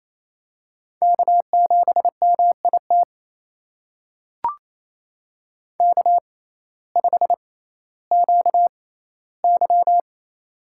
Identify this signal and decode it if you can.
Morse code, heard as a keyed tone, about 27 wpm, spaced as for 15 wpm, 700 Hz. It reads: K7MIT K5QY
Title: USA Call Signs - 6x Character Spacing - 28wpm